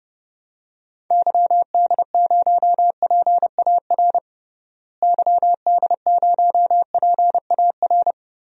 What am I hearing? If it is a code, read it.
Morse code, heard as a keyed tone, about 30 wpm, 700 Hz. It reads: YD0PAR YD0PAR
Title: DXCC-100 Call Signs - 6 Characters - Repeated 2x - 30wpm